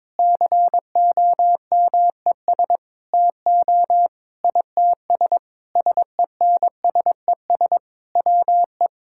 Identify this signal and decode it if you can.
Morse code, heard as a keyed tone, about 22 wpm, 700 Hz. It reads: COMES TO ITS SENSES WE
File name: Sentences with 9-Word Phrases - 22wpm